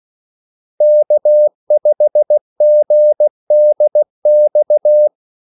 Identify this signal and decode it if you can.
Morse code, heard as a keyed tone, about 16 wpm, 600 Hz. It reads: K5GDX
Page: Kaldesignaler 16 wpm